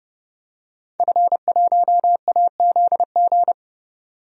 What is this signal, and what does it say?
Morse code, heard as a keyed tone, about 29 wpm, 700 Hz. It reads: F1AZG
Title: DXCC-100 Call Signs - 5 Characters - 30wpm